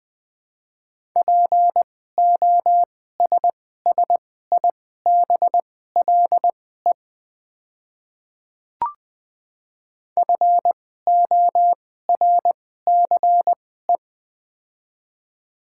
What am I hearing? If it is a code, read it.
Morse code, heard as a keyed tone, about 20 wpm, spaced as for 14 wpm, 700 Hz. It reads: POSSIBLE FORCE
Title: 2x Character Spacing - Top 500 Words - 20wpm